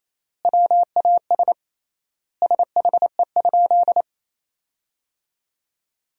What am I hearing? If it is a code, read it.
Morse code, heard as a keyed tone, about 28 wpm, 700 Hz. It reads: WAS SHE?
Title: 3x Word-Spacing - Sentences from Top 200 Words - 28wpm